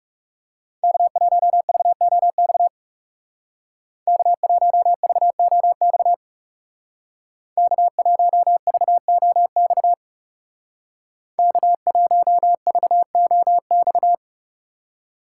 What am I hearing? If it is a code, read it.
Morse code, heard as a keyed tone, about 37 wpm, 700 Hz. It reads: K1VOX K1VOX K1VOX K1VOX
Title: Speed Racing - USA 1x3 Call Signs 30wpm